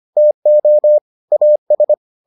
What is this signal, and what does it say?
Morse code, heard as a keyed tone, about 25 wpm, 600 Hz. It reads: TO AS